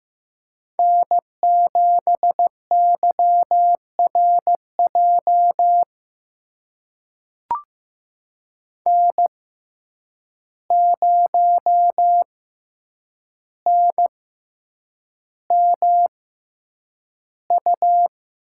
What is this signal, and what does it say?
Morse code, heard as a keyed tone, about 15 wpm, spaced as for 8 wpm, 700 Hz. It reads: N7YRJ N0NMU
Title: USA Call Signs - 5 Characters - 6x Character Spacing - 15wpm